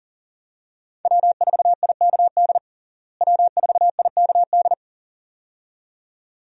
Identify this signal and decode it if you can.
Morse code, heard as a keyed tone, about 40 wpm, 700 Hz. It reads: W4IKD W4IKD
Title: USA Call Signs - 5 Characters - Repeated 2x - 40wpm